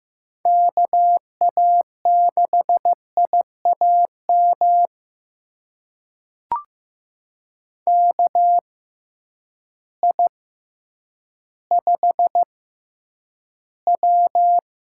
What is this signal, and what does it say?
Morse code, heard as a keyed tone, about 15 wpm, spaced as for 9 wpm, 700 Hz. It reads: KA6IAM KI5W
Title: USA Call Signs - 6x Character Spacing - 15wpm